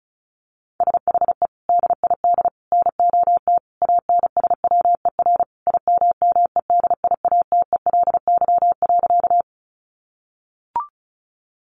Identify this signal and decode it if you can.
Morse code, heard as a keyed tone, about 34 wpm, 700 Hz. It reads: SHE DID NOT ANSWER IMMEDIATELY.
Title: Sentences from Top 1100 Words 35wpm